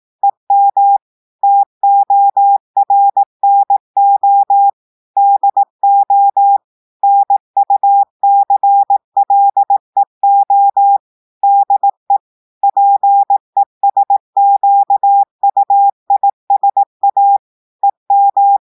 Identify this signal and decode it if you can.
Morse code, heard as a keyed tone, about 18 wpm, 800 Hz. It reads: EM TORNO DO NUCLEO DE PESQUISA EM
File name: audios / portugais / portugais texte en clair 1 vitesse 18 mots minutes